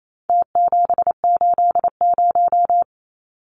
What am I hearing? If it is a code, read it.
Morse code, heard as a keyed tone, about 28 wpm, 700 Hz. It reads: T780
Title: Numbers - Contest Style - 4-Digit 28wpm